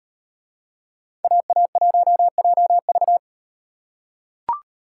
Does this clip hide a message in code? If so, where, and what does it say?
Yes: Morse code from 1.2 to 3.2 s, heard as a keyed tone, about 38 wpm, 700 Hz. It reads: AA1JV